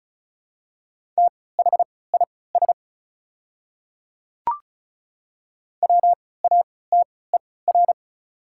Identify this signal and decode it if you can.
Morse code, heard as a keyed tone, about 36 wpm, spaced as for 20 wpm, 700 Hz. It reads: THIS WATER
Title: 3x Character Spacing - Top 100 Words - 35wpm